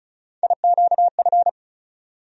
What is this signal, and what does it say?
Morse code, heard as a keyed tone, about 36 wpm, 700 Hz. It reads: IQF
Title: USA Call Sign Postfixes - 3 Letters 35wpm